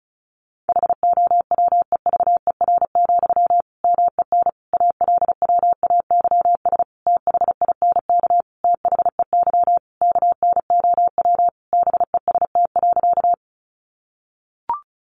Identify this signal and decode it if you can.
Morse code, heard as a keyed tone, about 34 wpm, 700 Hz. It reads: HOWEVER, MEN ALWAYS THINK THEY KNOW BEST.
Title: Sentences from Top 600 Words 35wpm